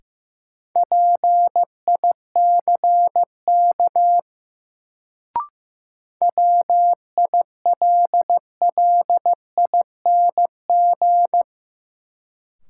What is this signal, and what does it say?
Morse code, heard as a keyed tone, about 15 wpm, 700 Hz. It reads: PICK WILLING
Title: Sentences from Top 1000 Words - New Word Drill - 15wpm